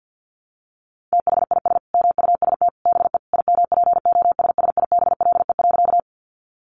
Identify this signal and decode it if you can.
Morse code, heard as a keyed tone, about 50 wpm, 700 Hz. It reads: THIS MUST BE IMPOSSIBLE.
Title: Sentences from Top 1100 Words 50wpm